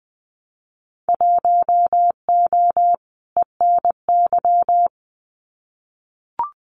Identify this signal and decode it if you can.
Morse code, heard as a keyed tone, about 20 wpm, 700 Hz. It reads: 1O ENY
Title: Winter Field Day Exchanges 20wpm